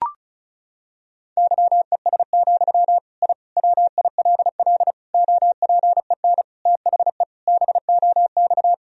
Morse code, heard as a keyed tone, about 36 wpm, 700 Hz. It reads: YES, I WILL OPEN THE BOX
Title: Sentences from Top 1000 Words - Review Part 2 - 35wpm